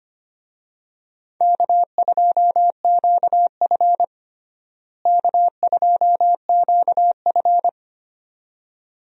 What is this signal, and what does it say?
Morse code, heard as a keyed tone, about 25 wpm, 700 Hz. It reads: K2QF K2QF
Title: USA Call Signs - 4 Characters - Repeated 2x - 25wpm